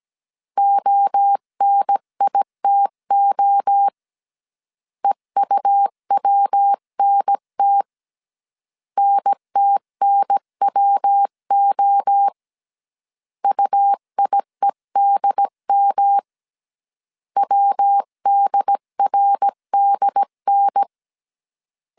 Morse code, heard as a keyed tone, about 17 wpm, 800 Hz. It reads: ONITO EUWNT NTNWO UIEDM WDRDN
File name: CW / DET / 1 3 1500C S A 16wpm 2141